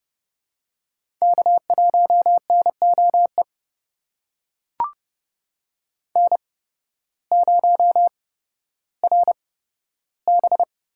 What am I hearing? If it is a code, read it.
Morse code, heard as a keyed tone, about 30 wpm, spaced as for 14 wpm, 700 Hz. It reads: K1NOE N0RB
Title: USA Call Signs - 5 Characters - 8x Character Spacing - 30wpm